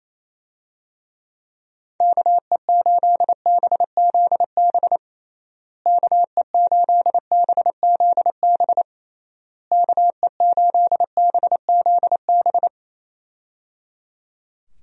Morse code, heard as a keyed tone, about 28 wpm, 700 Hz. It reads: KE8BZB KE8BZB KE8BZB
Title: USA Call Signs - 6 Characters - Repeated 3x - 28wpm